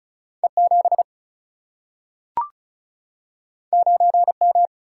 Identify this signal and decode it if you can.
Morse code, heard as a keyed tone, about 36 wpm, 700 Hz. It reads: E7 9M
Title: Single-Letter Single-Number 35wpm